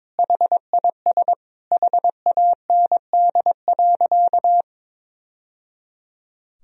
Morse code, heard as a keyed tone, about 22 wpm, 700 Hz. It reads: HIS HAND.
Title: Sentences from Top 1000 Words - Review Part 2 - 22wpm